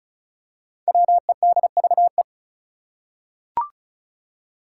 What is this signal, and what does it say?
Morse code, heard as a keyed tone, about 36 wpm, 700 Hz. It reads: WEDVE
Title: Five Challenging Letter-Number Combinations 35wpm